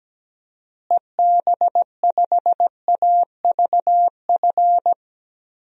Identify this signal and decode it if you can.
Morse code, heard as a keyed tone, about 17 wpm, 700 Hz. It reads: EB5AVF
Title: DXCC-100 Call Signs - 6 Characters - Repeated 2x - 17wpm